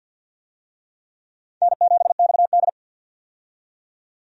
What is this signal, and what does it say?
Morse code, heard as a keyed tone, about 50 wpm, 700 Hz. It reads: N7XD
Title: USA Call Signs - 4 Characters - 50wpm